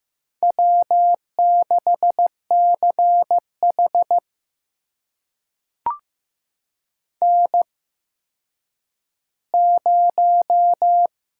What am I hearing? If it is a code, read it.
Morse code, heard as a keyed tone, about 15 wpm, spaced as for 9 wpm, 700 Hz. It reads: W6CH N0
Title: USA Call Signs - 4 Characters - 8x Character Spacing - 15wpm